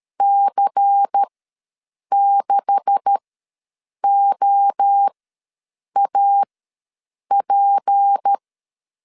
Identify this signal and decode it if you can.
Morse code, heard as a keyed tone, about 13 wpm, spaced as for 7 wpm, 800 Hz. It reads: C6OAP
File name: LEZIONI / 9 286 ws 12wpm 3351